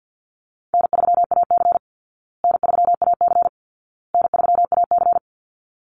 Morse code, heard as a keyed tone, about 48 wpm, 700 Hz. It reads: N3AC N3AC N3AC
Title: Easy Call Signs Repeated 3x - 50wpm